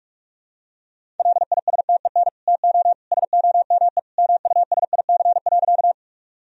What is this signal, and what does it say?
Morse code, heard as a keyed tone, about 46 wpm, 700 Hz. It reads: LISTEN TO SOME MUSIC.
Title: Sentences from Top 1000 Words - Review Part 1 - 45wpm